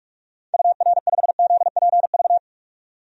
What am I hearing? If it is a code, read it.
Morse code, heard as a keyed tone, about 46 wpm, 700 Hz. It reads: UR5ZPV
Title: DXCC-100 Call Signs - 6 Characters - 45wpm